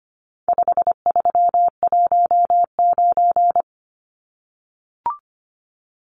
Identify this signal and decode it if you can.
Morse code, heard as a keyed tone, about 25 wpm, 700 Hz. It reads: 5319
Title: Numbers - 4-Digit 25wpm